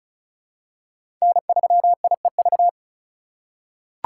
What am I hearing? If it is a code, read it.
Morse code, heard as a keyed tone, about 34 wpm, 700 Hz. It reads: N3IEV
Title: USA Call Signs - 2x Character Spacing - 35wpm